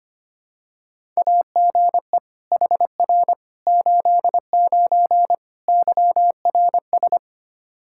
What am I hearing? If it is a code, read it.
Morse code, heard as a keyed tone, about 25 wpm, 700 Hz. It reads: AGE HR 89 YRS
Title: Practice QSO - Exchange 3 - More Info - 25wpm